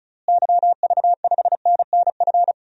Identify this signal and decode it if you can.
Morse code, heard as a keyed tone, about 34 wpm, 700 Hz. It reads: YV5NNF
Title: DXCC-100 Call Signs - 6 Characters - Repeated 3x - 35wpm